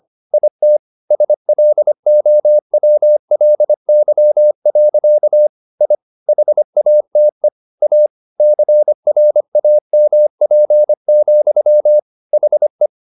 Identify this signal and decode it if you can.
Morse code, heard as a keyed tone, about 25 wpm, 600 Hz. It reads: IT SLOWLY. I HATE A CRAMP, HE
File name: the old man and the sea part 1 25wpm